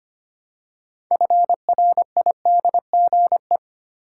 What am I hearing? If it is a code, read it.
Morse code, heard as a keyed tone, about 25 wpm, 690 Hz. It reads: FRIDGE
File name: practice / 25wpm / Easy Words with Hints - 25wpm